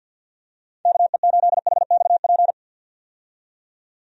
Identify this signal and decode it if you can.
Morse code, heard as a keyed tone, about 51 wpm, 700 Hz. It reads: KE8HXP
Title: USA Call Signs - 6 Characters - 50wpm